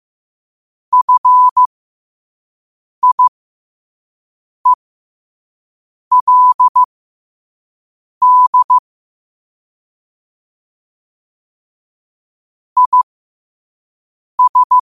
Morse code, heard as a keyed tone, about 15 wpm, spaced as for 5 wpm, 1000 Hz. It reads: FIELD IS
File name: qotdfiles / 5wpm